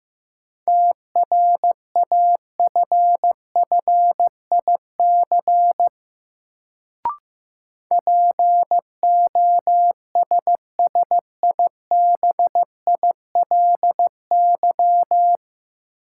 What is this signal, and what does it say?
Morse code, heard as a keyed tone, about 15 wpm, 700 Hz. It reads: TRAFFIC POSSIBILY
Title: Sentences from Top 1300 Words - New Word Drill - 15wpm